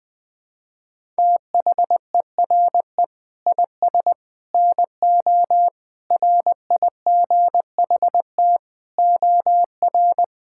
Morse code, heard as a keyed tone, about 20 wpm, 700 Hz. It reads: THERE IS NO RIGHT OR
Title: Sentences with 8-Word Phrases - 20wpm